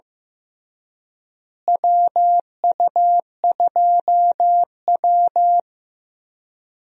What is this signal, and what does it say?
Morse code, heard as a keyed tone, about 15 wpm, 700 Hz. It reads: WU2W
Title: USA Call Signs - 4 Characters - Repeated 3x - 15wpm